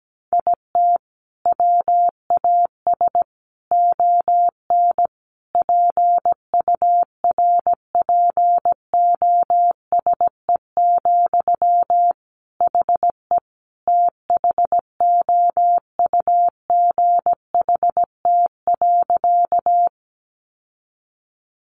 Morse code, heard as a keyed tone, about 17 wpm, 700 Hz. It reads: IT WAS ON PURPOSE, HE THOUGHT.
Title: Sentences from Top 800 Words 17wpm